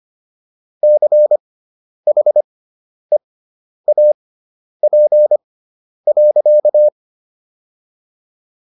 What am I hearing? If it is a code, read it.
Morse code, heard as a keyed tone, about 25 wpm, spaced as for 10 wpm, 600 Hz. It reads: CHEAP.